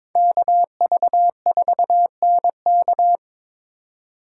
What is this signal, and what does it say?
Morse code, heard as a keyed tone, about 22 wpm, 700 Hz. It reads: KV4NK
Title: USA Call Signs - 4x Character Spacing - 22wpm